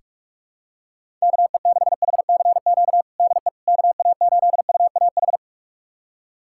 Morse code, heard as a keyed tone, about 42 wpm, 700 Hz. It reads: KE6HCQ DE KA8UAH
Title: USA Callsign DE Callsign 45wpm